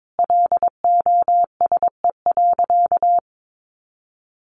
Morse code, heard as a keyed tone, about 22 wpm, 700 Hz. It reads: LOSE.
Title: Sentences from Top 1100 Words 22wpm